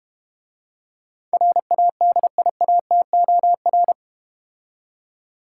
Morse code, heard as a keyed tone, about 30 wpm, 700 Hz. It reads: RADIATOR
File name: practice / 25wpm / Difficult Words - Speed-Racing with Hints - 25wpm